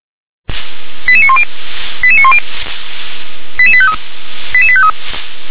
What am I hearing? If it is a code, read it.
Five-tone selective call (ZVEI), heard as quick runs of five tones, tones of about 70 ms. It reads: 88910 88910 88853 88853